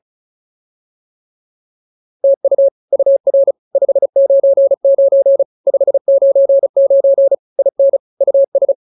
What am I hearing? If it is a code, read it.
Morse code, heard as a keyed tone, about 34 wpm, 550 Hz. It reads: TU UR 599 599 IN US